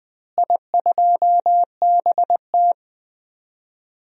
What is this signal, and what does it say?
Morse code, heard as a keyed tone, about 20 wpm, 700 Hz. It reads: I2BT
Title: DXCC-100 Call Signs - 4 Characters - Repeated 3x - 20wpm